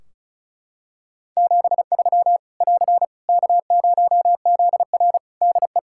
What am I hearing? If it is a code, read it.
Morse code, heard as a keyed tone, about 36 wpm, 700 Hz. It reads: 73 <AR> K0ZR DE
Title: K1USN SST QSO Practice 35wpm